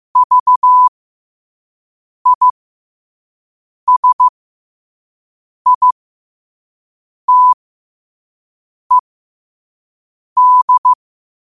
Morse code, heard as a keyed tone, about 15 wpm, spaced as for 5 wpm, 1000 Hz. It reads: VISITED
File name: qotdfiles / 5wpm